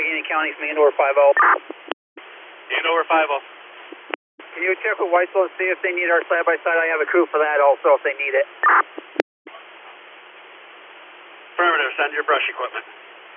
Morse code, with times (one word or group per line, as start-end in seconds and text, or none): none